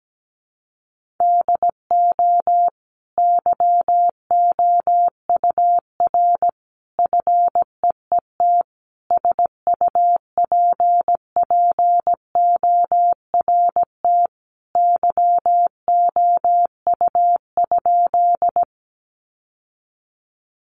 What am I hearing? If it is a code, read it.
Morse code, heard as a keyed tone, about 17 wpm, 700 Hz. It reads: DO YOUR FEET SUPPORT YOU?
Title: Sentences from Top 800 Words 17wpm